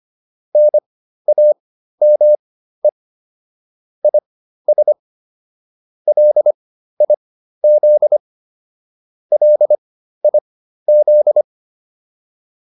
Morse code, heard as a keyed tone, about 25 wpm, spaced as for 13 wpm, 600 Hz. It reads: NAME IS LIZ LIZ